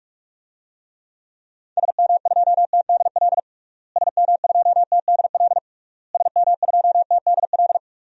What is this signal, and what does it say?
Morse code, heard as a keyed tone, about 41 wpm, 700 Hz. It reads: SM2TDL SM2TDL SM2TDL
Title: DXCC-100 Call Signs - 6 Characters - Repeated 3x - 45wpm